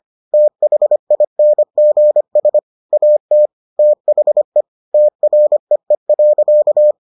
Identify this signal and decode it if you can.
Morse code, heard as a keyed tone, about 25 wpm, 600 Hz. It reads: THINGS AT THE TREE.